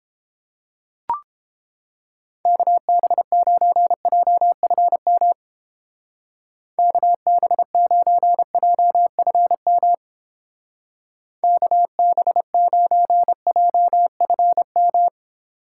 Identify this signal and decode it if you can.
Morse code, heard as a keyed tone, about 31 wpm, 700 Hz. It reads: KB9JFM KB9JFM KB9JFM